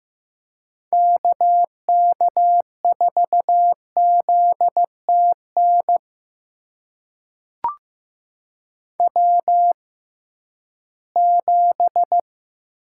Morse code, heard as a keyed tone, about 15 wpm, spaced as for 11 wpm, 700 Hz. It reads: KK4ZTN W7
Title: USA Call Signs - 6x Character Spacing - 15wpm